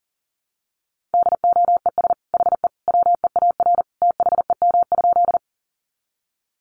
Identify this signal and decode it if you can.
Morse code, heard as a keyed tone, about 40 wpm, 700 Hz. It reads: DOES HE WEAR THEM?
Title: Sentences from Top 1500 Words 40wpm